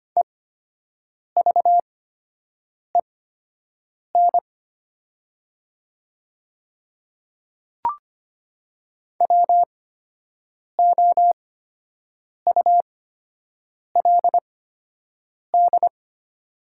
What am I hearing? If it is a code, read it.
Morse code, heard as a keyed tone, about 25 wpm, spaced as for 7 wpm, 700 Hz. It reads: EVEN WOULD
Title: 8x Character Spacing - Top 200 Words - 25wpm